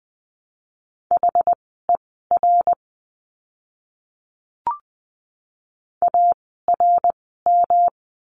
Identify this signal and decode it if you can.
Morse code, heard as a keyed tone, about 20 wpm, spaced as for 15 wpm, 700 Hz. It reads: HER ARM
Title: 108 Lesson 18f - Letter m - Words 20wpm